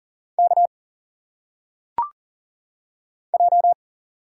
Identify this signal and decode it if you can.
Morse code, heard as a keyed tone, about 40 wpm, 700 Hz. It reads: K J
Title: Single Letters - Phonetic Pronunciation 40wpm